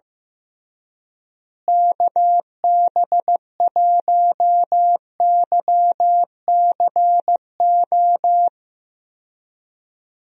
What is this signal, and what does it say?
Morse code, heard as a keyed tone, about 16 wpm, 700 Hz. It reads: KB1YCO
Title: USA Call Signs - 6 Characters - Repeated 2x - 15wpm